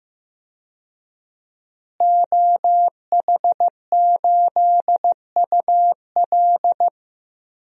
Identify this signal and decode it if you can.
Morse code, heard as a keyed tone, about 15 wpm, 700 Hz. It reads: OH8UL
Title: DXCC-100 Call Signs - 5 Characters - Repeated 2x - 15wpm